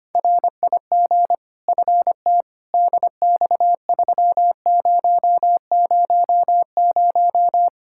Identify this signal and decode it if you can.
Morse code, heard as a keyed tone, about 25 wpm, 700 Hz. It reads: RIG FT DX3000
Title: K1USN SST QSO Practice 25wpm